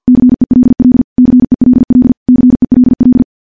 Morse code, heard as a keyed tone, about 49 wpm, 250 Hz. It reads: CELL CELL CELL
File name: morse / morse-training-50WPM